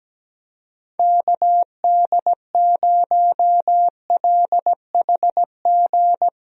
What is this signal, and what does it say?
Morse code, heard as a keyed tone, about 17 wpm, 700 Hz. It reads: KD0LHG
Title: USA Call Signs - 6 Characters - Repeated 2x - 17wpm